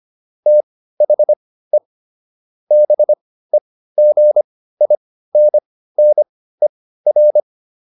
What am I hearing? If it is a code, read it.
Morse code, heard as a keyed tone, about 25 wpm, spaced as for 15 wpm, 600 Hz. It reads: THE BEGINNER